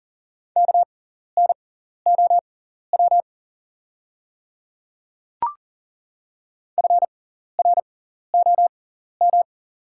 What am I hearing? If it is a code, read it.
Morse code, heard as a keyed tone, about 39 wpm, spaced as for 14 wpm, 690 Hz. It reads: KNOW FROM
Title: 6x Character Spacing - Top 100 Words - 40wpm